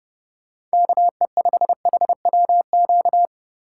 Morse code, heard as a keyed tone, about 27 wpm, 700 Hz. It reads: KE5HWQ